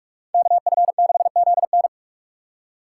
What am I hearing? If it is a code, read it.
Morse code, heard as a keyed tone, about 59 wpm, spaced as for 51 wpm, 700 Hz. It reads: KF6ZN
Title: Speed Racing - USA Call Signs 45wpm